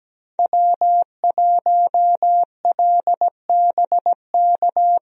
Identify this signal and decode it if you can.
Morse code, heard as a keyed tone, about 17 wpm, 700 Hz. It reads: W1LBK
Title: USA Call Signs - 5 Characters - Repeated 3x - 17wpm